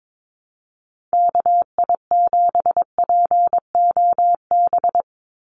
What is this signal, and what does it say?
Morse code, heard as a keyed tone, about 22 wpm, 700 Hz. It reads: KI7POB
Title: USA Call Signs 22wpm